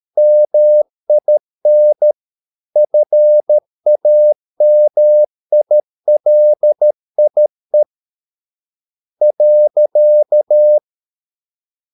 Morse code, heard as a keyed tone, about 13 wpm, spaced as for 11 wpm, 600 Hz. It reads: MIN FAMILIE.
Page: Ferie 13 wpm